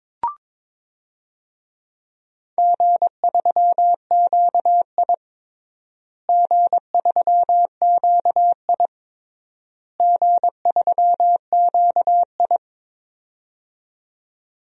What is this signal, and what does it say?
Morse code, heard as a keyed tone, about 22 wpm, 700 Hz. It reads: G3QI G3QI G3QI
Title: DXCC-100 Call Signs - 4 Characters - Repeated 3x - 22wpm